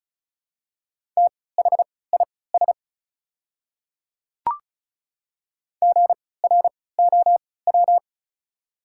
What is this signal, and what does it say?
Morse code, heard as a keyed tone, about 36 wpm, spaced as for 20 wpm, 700 Hz. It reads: THIS GROW
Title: 3x Character Spacing - Top 500 Words - 35wpm